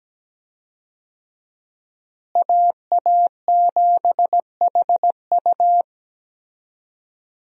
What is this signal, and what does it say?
Morse code, heard as a keyed tone, about 17 wpm, 700 Hz. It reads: AA7HU
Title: USA Call Signs - 5 Characters - Repeated 2x - 17wpm